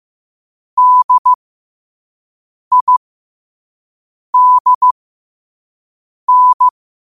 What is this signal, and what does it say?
Morse code, heard as a keyed tone, about 15 wpm, spaced as for 5 wpm, 1000 Hz. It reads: DIDN
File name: qotdfiles / 5wpm